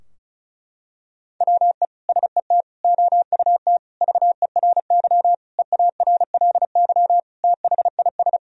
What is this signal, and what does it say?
Morse code, heard as a keyed tone, about 34 wpm, 700 Hz. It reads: WE SET OUT VERY EARLY THIS
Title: Sentences from Top 1000 Words - Review Part 2 - 35wpm